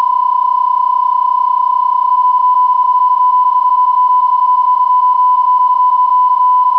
PSK31, about 31.25 baud.